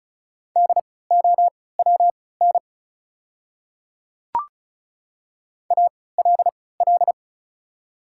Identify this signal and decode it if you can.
Morse code, heard as a keyed tone, about 36 wpm, spaced as for 20 wpm, 690 Hz. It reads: DOWN ALL
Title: 3x Character Spacing - Top 100 Words - 35wpm